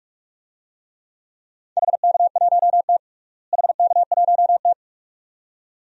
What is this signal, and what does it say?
Morse code, heard as a keyed tone, about 43 wpm, 700 Hz. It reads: HK1T HK1T
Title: DXCC-100 Call Signs - 4 Characters - Repeated 2x - 45wpm